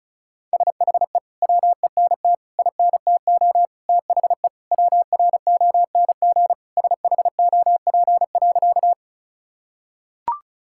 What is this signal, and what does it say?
Morse code, heard as a keyed tone, about 34 wpm, 700 Hz. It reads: SHE WENT INTO THE WRONG SHOP.
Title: Sentences from Top 1600 Words 35wpm